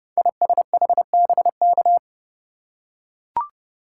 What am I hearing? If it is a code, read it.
Morse code, heard as a keyed tone, about 30 wpm, 700 Hz. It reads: ISHBK
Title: Code Groups - 5 Letters - 30wpm